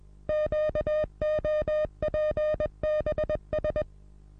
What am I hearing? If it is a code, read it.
Morse code, heard as a keyed tone, about 21 wpm, 610 Hz. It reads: QOPBS
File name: repos / Morsekurs / DJ4UF Morsen26